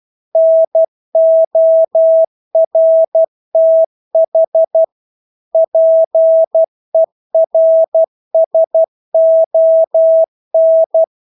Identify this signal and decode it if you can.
Morse code, heard as a keyed tone, about 12 wpm, 650 Hz. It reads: NORTH PERSON
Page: Engelske ord 12 wpm